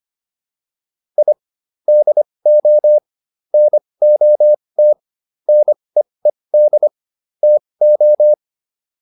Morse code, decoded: I DO NOT NEED TO

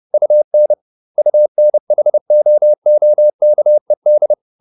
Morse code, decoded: UN UNHOOKED

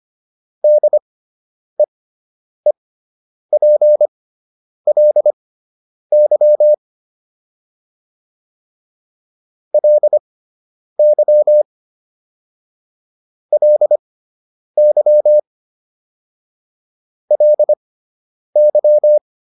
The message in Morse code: DEEPLY LY LY LY

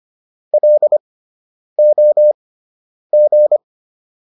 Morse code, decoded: LOG

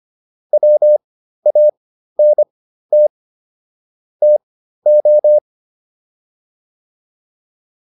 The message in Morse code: WANT TO

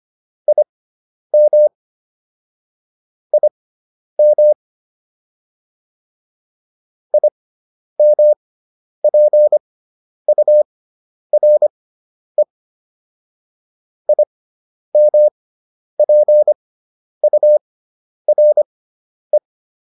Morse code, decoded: IM IM IMPURE IMPURE

